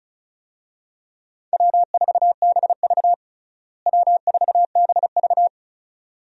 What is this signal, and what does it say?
Morse code, heard as a keyed tone, about 34 wpm, 700 Hz. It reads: W4BV W4BV
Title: USA Call Signs - 4 Characters - Repeated 2x - 35wpm